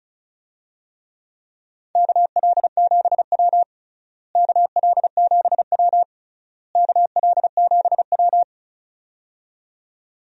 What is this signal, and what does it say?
Morse code, heard as a keyed tone, about 36 wpm, 700 Hz. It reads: KL7W KL7W KL7W
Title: DXCC-100 Call Signs - 4 Characters - Repeated 3x - 35wpm